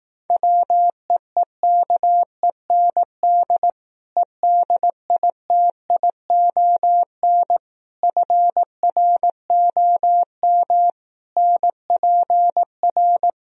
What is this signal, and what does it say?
Morse code, heard as a keyed tone, about 18 wpm, 700 Hz. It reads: WEEKEND EDITION FROM NPR
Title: Speed-Racing - Sets of 5 Words 15wpm